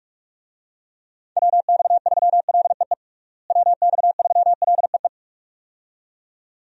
Morse code, heard as a keyed tone, about 47 wpm, 700 Hz. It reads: WX3LEE WX3LEE
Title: USA Call Signs - 6 Characters - Repeated 2x - 45wpm